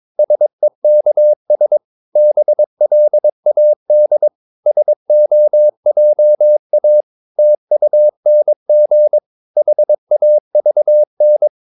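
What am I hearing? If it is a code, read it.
Morse code, heard as a keyed tone, about 22 wpm, 600 Hz. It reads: SEKS BLAD SOJA TUNG HAVN
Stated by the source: Morse practice of short Danish words at 22 wpm, from CW med Gnister.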